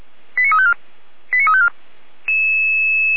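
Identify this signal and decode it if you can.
Five-tone selective call (ZVEI), heard as quick runs of five tones, tones of about 80 ms. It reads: 89354 89354
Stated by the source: ZVEI five-tone selective call received with an SDR.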